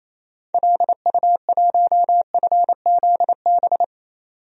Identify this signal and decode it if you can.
Morse code, heard as a keyed tone, about 28 wpm, 700 Hz. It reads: LU1FZB
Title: DXCC-100 Call Signs - 6 Characters - 28wpm